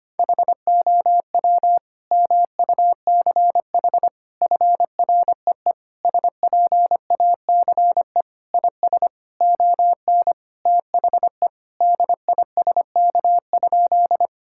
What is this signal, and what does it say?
Morse code, heard as a keyed tone, about 25 wpm, 700 Hz. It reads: HOW MUCH FREE SPACE IS ON THE DISK?
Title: Sentences from Top 1000 Words - Review Part 6 - 25wpm